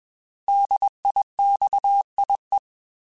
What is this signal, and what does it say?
Morse code, heard as a keyed tone, about 21 wpm, 800 Hz. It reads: DIXIE
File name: Alphabetical / Alphabetical mp3s / Ds @20